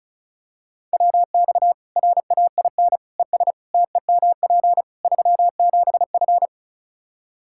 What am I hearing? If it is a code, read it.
Morse code, heard as a keyed tone, about 36 wpm, 700 Hz. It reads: WX RAIN ES TEMP 37F
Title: Practice QSO - Exchange 2 - Info - 35wpm